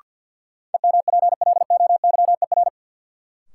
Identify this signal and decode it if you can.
Morse code, heard as a keyed tone, about 49 wpm, 700 Hz. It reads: EMPLOYER
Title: Sentences from Top 1600 Words - New Word Drill - 50wpm